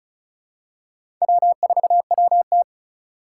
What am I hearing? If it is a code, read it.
Morse code, heard as a keyed tone, about 34 wpm, 700 Hz. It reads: W4WT
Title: USA Call Signs - 3x Character Spacing - 35wpm